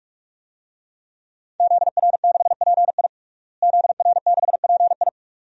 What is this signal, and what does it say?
Morse code, heard as a keyed tone, about 42 wpm, 700 Hz. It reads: ZR6PI ZR6PI
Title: DXCC-100 Call Signs - 5 Characters - Repeated 2x - 45wpm